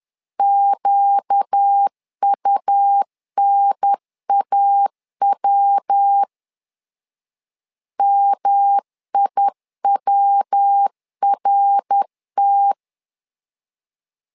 Morse code, heard as a keyed tone, about 10 wpm, 800 Hz. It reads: QUNAW MIWRT
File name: LEZIONI / 4 Q 10wpm 2210